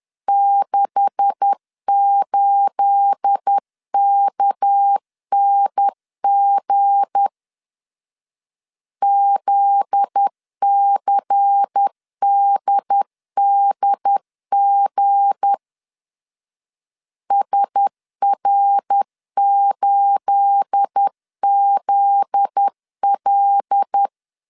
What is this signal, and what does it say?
Morse code, heard as a keyed tone, about 10 wpm, 800 Hz. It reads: 68KNG ZCDDG SR8ZL
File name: CW / DET / 1 9 1500C S B 10wpm 4439